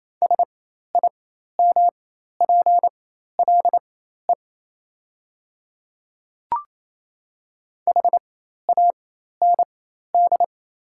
Morse code, heard as a keyed tone, about 28 wpm, spaced as for 13 wpm, 700 Hz. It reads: SIMPLE HAND
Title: 4x Character Spacing - Top 500 Words - 28wpm